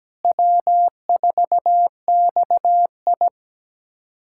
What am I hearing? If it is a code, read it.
Morse code, heard as a keyed tone, about 17 wpm, 700 Hz. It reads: W4XI
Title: USA Callsign DE Callsign - 4 Characters - 17wpm